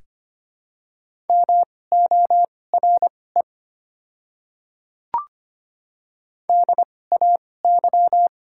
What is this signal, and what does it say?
Morse code, heard as a keyed tone, about 25 wpm, spaced as for 18 wpm, 690 Hz. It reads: MORE DAY
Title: 2x Character Spacing - Top 100 Words - 25wpm